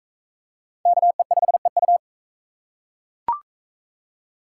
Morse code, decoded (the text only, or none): KE5EU